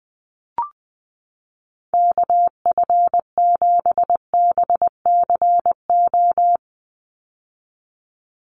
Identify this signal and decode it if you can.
Morse code, heard as a keyed tone, about 20 wpm, 700 Hz. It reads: KF7BCO
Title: USA Call Signs 20wpm